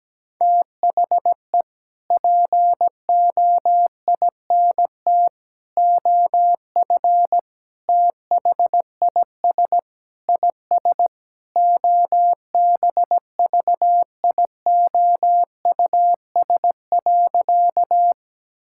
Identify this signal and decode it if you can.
Morse code, heard as a keyed tone, about 17 wpm, 700 Hz. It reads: THE POINT OF THIS IS OBVIOUS.
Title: Sentences from Top 1600 Words 17wpm